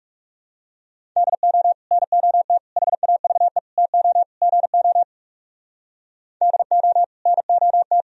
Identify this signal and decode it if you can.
Morse code, heard as a keyed tone, about 39 wpm, 700 Hz. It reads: DO NOT HAVE TO GO DO NOT